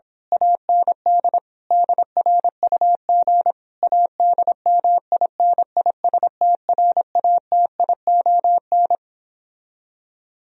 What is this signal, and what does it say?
Morse code, heard as a keyed tone, about 26 wpm, 700 Hz. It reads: AND DRUG ADMINISTRATION
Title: Speed-Racing - Sets of 5 Words 22wpm